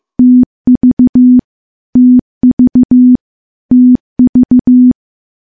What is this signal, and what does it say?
Morse code, heard as a keyed tone, about 15 wpm, 250 Hz. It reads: TV TV TV
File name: morse-training-15WPM